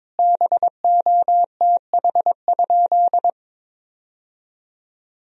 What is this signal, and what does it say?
Morse code, heard as a keyed tone, about 22 wpm, 700 Hz. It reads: BOTH?
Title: Sentences from Top 1000 Words - Review Part 1 - 22wpm